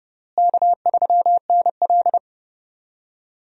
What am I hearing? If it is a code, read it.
Morse code, heard as a keyed tone, about 29 wpm, 700 Hz. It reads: K3NL
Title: USA Call Signs - 4 Characters - Repeated 3x - 30wpm